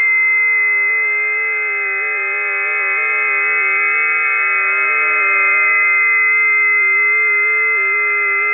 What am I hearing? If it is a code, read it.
FT8.
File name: tests / 20m busy / test 32